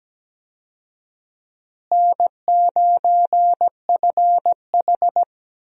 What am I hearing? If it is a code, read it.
Morse code, heard as a keyed tone, about 17 wpm, 700 Hz. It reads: N9FH
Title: USA Call Signs - 4 Characters - Repeated 3x - 17wpm